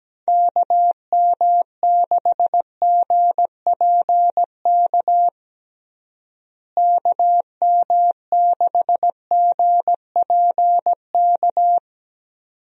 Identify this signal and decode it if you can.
Morse code, heard as a keyed tone, about 17 wpm, 700 Hz. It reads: KM6GPK KM6GPK
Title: USA Call Signs - 6 Characters - Repeated 2x - 17wpm